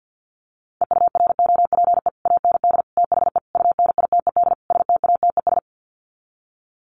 Morse code, heard as a keyed tone, about 50 wpm, 700 Hz. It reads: EUROPE AND THE UNITED STATES
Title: Sets of 5 Words Encore2 50wpm